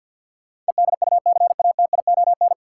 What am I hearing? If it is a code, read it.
Morse code, heard as a keyed tone, about 52 wpm, 700 Hz. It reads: EDUCATION
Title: Sentences from Top 1100 Words - New Word Drill Alt - 50wpm